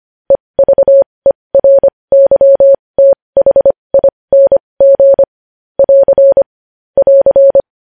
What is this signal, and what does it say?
Morse code, heard as a keyed tone, about 25 wpm, spaced as for 20 wpm, 550 Hz. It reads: EVERYTHING <AR> <AR>
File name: podcast / News headlines-1x-25x20